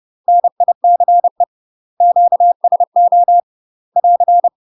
Morse code, heard as a keyed tone, about 30 wpm, 700 Hz. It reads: NICE QSO <AR>
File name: UK MASTER QSOs TRAINING / 30WPM / UK MASTER QSOs 0015